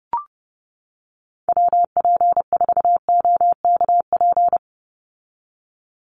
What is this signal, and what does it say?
Morse code, heard as a keyed tone, about 30 wpm, 700 Hz. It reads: WP4OKP
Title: USA Call Signs 30wpm